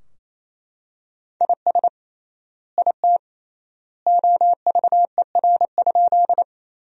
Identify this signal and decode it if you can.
Morse code, heard as a keyed tone, about 28 wpm, 700 Hz. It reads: IS IT OVER?
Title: 3x Word-Spacing - Sentences from Top 100 Words - 28wpm